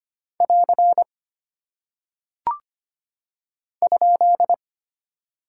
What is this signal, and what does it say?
Morse code, heard as a keyed tone, about 25 wpm, 700 Hz. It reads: <AR> ?
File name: practice / 25wpm / Single Punctuation-Prosign - 25wpm